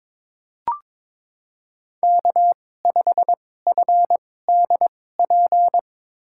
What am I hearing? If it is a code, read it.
Morse code, heard as a keyed tone, about 21 wpm, spaced as for 16 wpm, 700 Hz. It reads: K5FDP
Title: USA Call Signs - 5 Characters - 2x Character Spacing - 22wpm